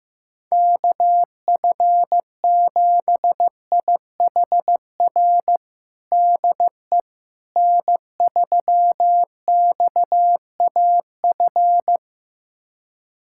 Morse code, heard as a keyed tone, about 16 wpm, 700 Hz. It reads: KF7IHR DE N3XAF
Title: USA Callsign DE Callsign 15wpm